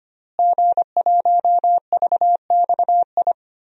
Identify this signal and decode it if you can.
Morse code, heard as a keyed tone, about 25 wpm, 700 Hz. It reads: G1VXI
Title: DXCC-100 Call Signs - 5 Characters - Repeated 3x - 25wpm